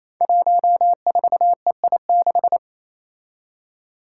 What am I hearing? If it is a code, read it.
Morse code, heard as a keyed tone, about 28 wpm, 700 Hz. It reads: 14EI6
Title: Five Challenging Letter-Number Combinations 28wpm